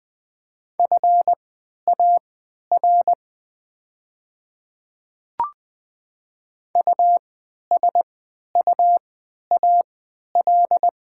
Morse code, heard as a keyed tone, about 20 wpm, spaced as for 11 wpm, 700 Hz. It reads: FAR USUAL
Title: 3x Character Spacing - Top 500 Words - 20wpm